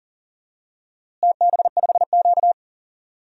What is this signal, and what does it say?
Morse code, heard as a keyed tone, about 40 wpm, 700 Hz. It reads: TB5Q
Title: DXCC-100 Call Signs - 4 Characters - Repeated 3x - 40wpm